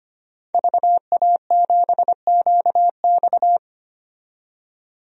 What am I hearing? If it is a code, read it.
Morse code, heard as a keyed tone, about 25 wpm, 700 Hz. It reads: VA7QX
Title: DXCC-100 Call Signs - 5 Characters - 25wpm